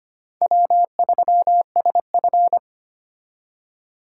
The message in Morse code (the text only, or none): W3SF